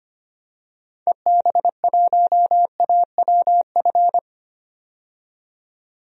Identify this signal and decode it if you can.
Morse code, heard as a keyed tone, about 25 wpm, 700 Hz. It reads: EB1AWF
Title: DXCC-100 Call Signs - 6 Characters - 25wpm